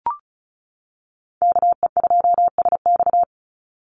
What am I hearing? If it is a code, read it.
Morse code, heard as a keyed tone, about 34 wpm, 700 Hz. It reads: KE2SX
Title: Call Signs - 35wpm